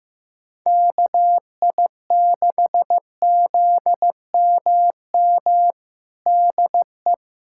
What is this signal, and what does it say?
Morse code, heard as a keyed tone, about 15 wpm, 700 Hz. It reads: KI6ZMM DE
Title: USA Callsign DE Callsign 15wpm